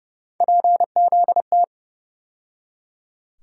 Morse code, heard as a keyed tone, about 32 wpm, 700 Hz. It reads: PZT
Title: USA Call Sign Postfixes - 3 Letters 30wpm